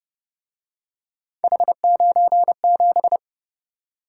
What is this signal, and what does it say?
Morse code, heard as a keyed tone, about 29 wpm, 700 Hz. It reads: H97
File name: Speed-Racing - Three Letter-Number Combinations - 25wpm